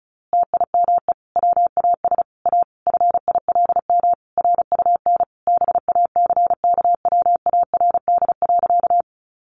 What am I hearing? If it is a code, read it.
Morse code, heard as a keyed tone, about 34 wpm, 700 Hz. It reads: TIME WAS A FILM RUN BACKWARD.
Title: A Sound Of Thunder 35wpm